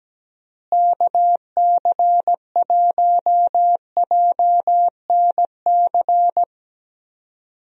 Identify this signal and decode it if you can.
Morse code, heard as a keyed tone, about 17 wpm, 700 Hz. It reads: KC1JNC
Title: USA Call Signs - 6 Characters - Repeated 2x - 17wpm